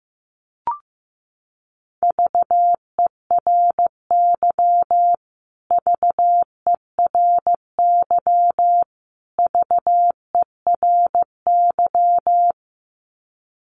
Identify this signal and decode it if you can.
Morse code, heard as a keyed tone, about 15 wpm, 700 Hz. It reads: VERY VERY VERY
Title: Top 500 Words Repeated 15wpm